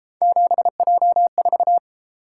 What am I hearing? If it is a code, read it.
Morse code, heard as a keyed tone, about 30 wpm, 700 Hz. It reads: 7J4